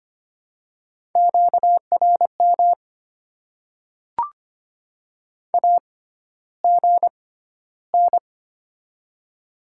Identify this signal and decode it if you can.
Morse code, heard as a keyed tone, about 25 wpm, spaced as for 13 wpm, 700 Hz. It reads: QRM AGN
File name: Top 90 QSO Elements - 6x Character Spacing - 25wpm